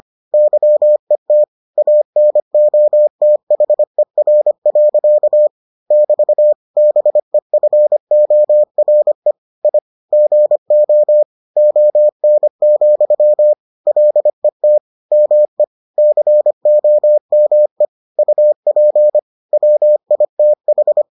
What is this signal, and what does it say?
Morse code, heard as a keyed tone, about 25 wpm, 600 Hz. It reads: YET ANOTHER. <BT> BEFORE I GO ON, LET ME COME UP WITH